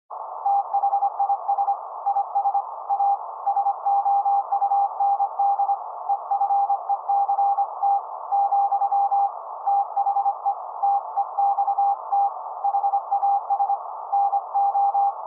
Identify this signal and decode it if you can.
Morse code, heard as a keyed tone, about 25 wpm, 800 Hz. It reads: THIS IS A SOUND EFECT , THE TEXT HAS NO
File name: se / pre2 / Morse Code02-3(Fast-Noise)